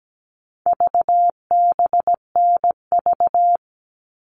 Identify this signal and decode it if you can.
Morse code, heard as a keyed tone, about 17 wpm, 700 Hz. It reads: VBNV
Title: All Cut Numbers - 4-Digit 17wpm